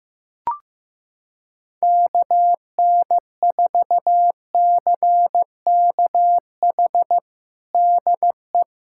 Morse code, heard as a keyed tone, about 15 wpm, 700 Hz. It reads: KN4CKH DE
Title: USA Callsign DE Callsign 15wpm